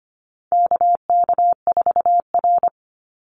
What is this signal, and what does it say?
Morse code, heard as a keyed tone, about 25 wpm, 700 Hz. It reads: KK4R
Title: Call Signs Repeated 2x - 25wpm